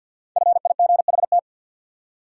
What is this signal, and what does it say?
Morse code, heard as a keyed tone, about 47 wpm, 700 Hz. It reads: FIGHT
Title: Five Letter Words - 50wpm